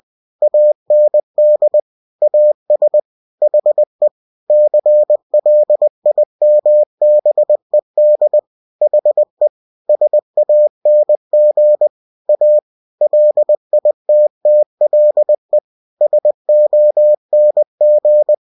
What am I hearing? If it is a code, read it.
Morse code, heard as a keyed tone, about 20 wpm, 600 Hz. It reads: AND AS HE CLIMBED HE SANG A LITTLE SONG
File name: cw-books / winnie-the-pooh / winnie-20wpm-page-0003